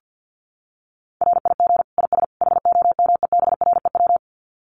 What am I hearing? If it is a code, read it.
Morse code, heard as a keyed tone, about 50 wpm, 700 Hz. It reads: RIG IS HOMEBREW
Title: Common QSO Phrases 50wpm